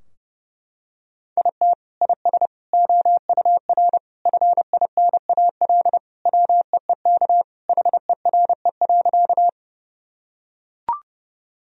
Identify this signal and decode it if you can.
Morse code, heard as a keyed tone, about 30 wpm, 700 Hz. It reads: IT IS OUR FINAL WEEK HERE.
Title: Sentences from Top 1000 Words - Review Part 2 - 30wpm